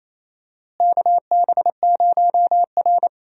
Morse code, heard as a keyed tone, about 28 wpm, 700 Hz. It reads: KB0R
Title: USA Call Signs - 4 Characters - Repeated 3x - 28wpm